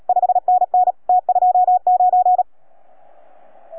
Morse code, heard as a keyed tone, about 37 wpm, 720 Hz. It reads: 5NN T29